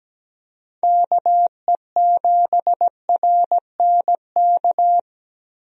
Morse code, heard as a keyed tone, about 17 wpm, 700 Hz. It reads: KE7RNK
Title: USA Call Signs - 6 Characters - Repeated 2x - 17wpm